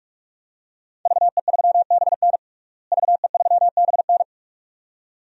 Morse code, heard as a keyed tone, about 47 wpm, 700 Hz. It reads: VE3BN VE3BN
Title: DXCC-100 Call Signs - 5 Characters - Repeated 2x - 45wpm